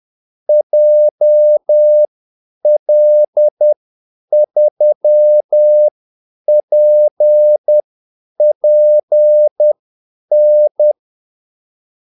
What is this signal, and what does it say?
Morse code, heard as a keyed tone, about 10 wpm, spaced as for 8 wpm, 600 Hz. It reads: JL3PPN
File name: Callsigns-10-8wpm